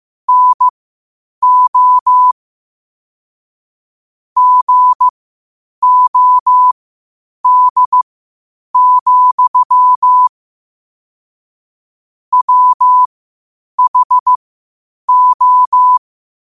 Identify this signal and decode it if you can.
Morse code, heard as a keyed tone, about 15 wpm, spaced as for 8 wpm, 1000 Hz. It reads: NO GOD, WHO
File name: qotdfiles / 8wpm